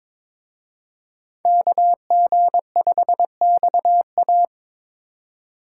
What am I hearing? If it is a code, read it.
Morse code, heard as a keyed tone, about 26 wpm, 700 Hz. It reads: KG5XA
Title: Speed Racing - USA 2x2 Call Signs 22wpm